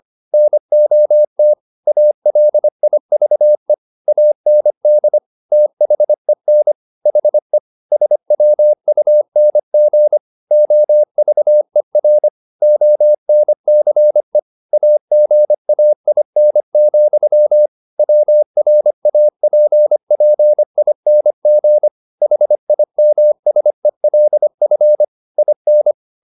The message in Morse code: NOT ALIVE AND THEN HE SWUNG OVER ONCE AGAIN, WRAPPING HIMSELF IN

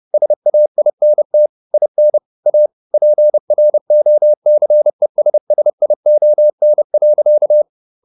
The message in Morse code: SAINT IN A PROCESSION.

